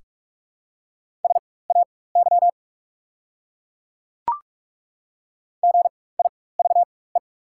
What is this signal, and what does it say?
Morse code, heard as a keyed tone, about 44 wpm, spaced as for 21 wpm, 690 Hz. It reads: SAY GIVE
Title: 4x Character Spacing - Top 200 Words - 45wpm